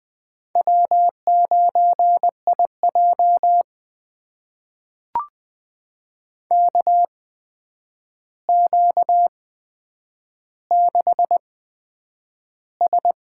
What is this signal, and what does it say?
Morse code, heard as a keyed tone, about 20 wpm, spaced as for 9 wpm, 700 Hz. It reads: W9IJ KQ6S
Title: USA Call Signs - 4 Characters - 8x Character Spacing - 20wpm